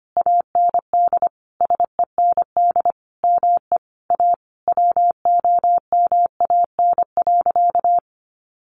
Morse code, heard as a keyed tone, about 25 wpm, 700 Hz. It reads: AND SEND ME A WOMAN.
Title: Sentences from Top 1500 Words 25wpm